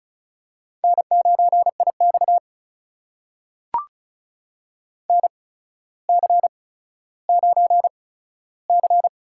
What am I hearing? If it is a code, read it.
Morse code, heard as a keyed tone, about 36 wpm, spaced as for 15 wpm, 700 Hz. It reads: N9IX NC9C
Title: USA Call Signs - 4 Characters - 8x Character Spacing - 35wpm